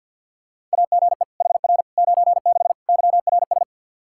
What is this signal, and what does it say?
Morse code, heard as a keyed tone, about 53 wpm, 700 Hz. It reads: AGE HR 96 YRS
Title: Practice QSO - Exchange 3 - More Info - 50wpm